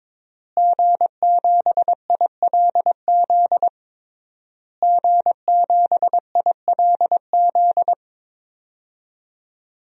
Morse code, heard as a keyed tone, about 22 wpm, 700 Hz. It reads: G7ILZ G7ILZ
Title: DXCC-100 Call Signs - 5 Characters - Repeated 3x - 22wpm